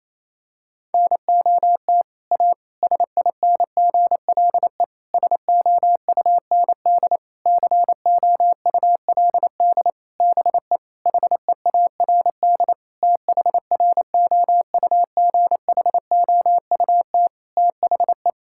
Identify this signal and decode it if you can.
Morse code, heard as a keyed tone, about 28 wpm, 700 Hz. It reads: NOT A SINGLE SOUND COULD BE HEARD THROUGHOUT THE
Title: Sentences from Top 1000 Words - Review Part 5 - 28wpm